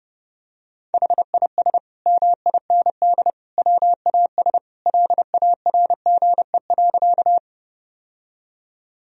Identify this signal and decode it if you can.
Morse code, heard as a keyed tone, about 32 wpm, 700 Hz. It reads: HIS MIND WAS LARGE.
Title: Sentences from Top 1000 Words - Review Part 1 - 30wpm